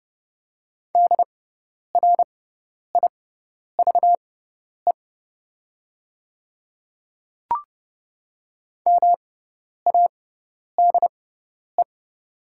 Morse code, heard as a keyed tone, about 30 wpm, spaced as for 10 wpm, 700 Hz. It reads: DRIVE MADE
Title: 6x Character Spacing - Top 500 Words - 30wpm